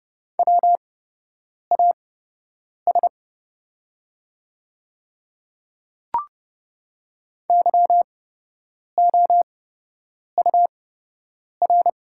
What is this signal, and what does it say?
Morse code, heard as a keyed tone, about 30 wpm, spaced as for 8 wpm, 700 Hz. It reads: WAS YOUR
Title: 8x Character Spacing - Top 100 Words - 30wpm